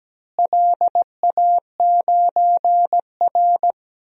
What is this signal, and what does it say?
Morse code, heard as a keyed tone, about 17 wpm, 700 Hz. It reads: LA9R
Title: DXCC-100 Call Signs - 4 Characters - Repeated 2x - 17wpm